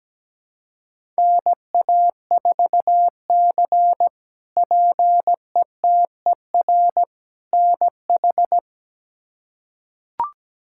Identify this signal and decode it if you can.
Morse code, heard as a keyed tone, about 17 wpm, 700 Hz. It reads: NA4C PETER NH
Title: K1USN SST QSO Practice - 17wpm